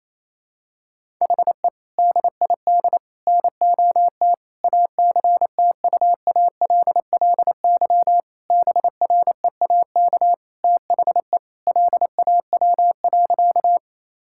Morse code, heard as a keyed tone, about 28 wpm, 700 Hz. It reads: HE DID NOT ACTUALLY BREAK THE LAW.
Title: Sentences from Top 1000 Words - Review Part 6 - 28wpm